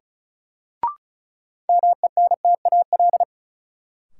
Morse code, heard as a keyed tone, about 36 wpm, 700 Hz. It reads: MENTAL
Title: Sentences from Top 1000 Words - Review Part 4 - New Word Drill - 35wpm